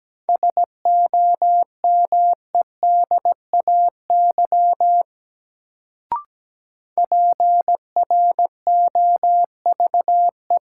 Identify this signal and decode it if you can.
Morse code, heard as a keyed tone, about 17 wpm, 700 Hz. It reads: SOMEDAY PROVE
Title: Sentences from Top 1600 Words - New Word Drill Alt - 17wpm